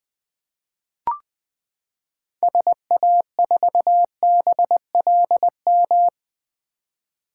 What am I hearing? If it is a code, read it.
Morse code, heard as a keyed tone, about 21 wpm, 700 Hz. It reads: SA4BLM
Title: 2023 SOTA Chaser Call Signs - 20wpm